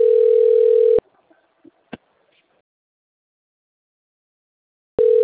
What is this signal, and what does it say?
Ringback tone (North America), 440 + 480 Hz.